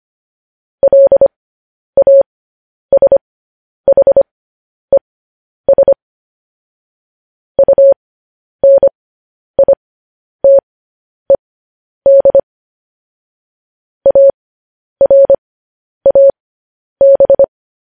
Morse code, heard as a keyed tone, about 25 wpm, spaced as for 10 wpm, 550 Hz. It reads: LASHES UNITED ARAB